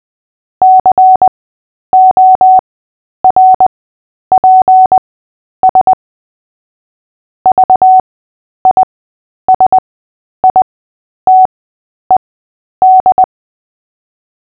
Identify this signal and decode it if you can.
Morse code, heard as a keyed tone, about 20 wpm, spaced as for 10 wpm, 750 Hz. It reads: CORPS VISITED
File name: Short-Story-7-10wpm